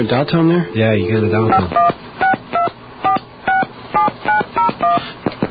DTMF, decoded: 5162467901